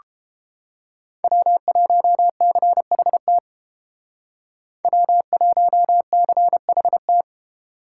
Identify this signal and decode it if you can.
Morse code, heard as a keyed tone, about 29 wpm, 700 Hz. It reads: W1CHT W1CHT